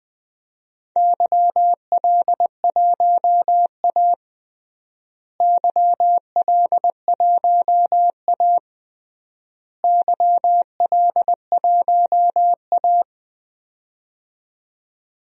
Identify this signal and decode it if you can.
Morse code, heard as a keyed tone, about 20 wpm, 700 Hz. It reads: YL1A YL1A YL1A
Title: DXCC-100 Call Signs - 4 Characters - Repeated 3x - 20wpm